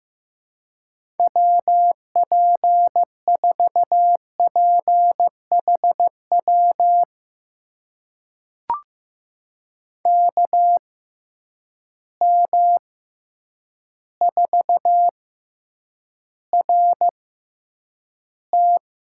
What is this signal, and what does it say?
Morse code, heard as a keyed tone, about 15 wpm, spaced as for 8 wpm, 700 Hz. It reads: WP4PHW KM4RT
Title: USA Call Signs - 6 Characters - 6x Character Spacing - 15wpm